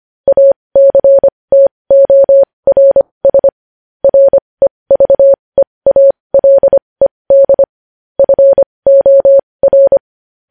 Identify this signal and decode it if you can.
Morse code, heard as a keyed tone, about 25 wpm, spaced as for 20 wpm, 550 Hz. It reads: ACTORS REVEALED FOR
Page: News Headlines in Morse Code at 20 WPM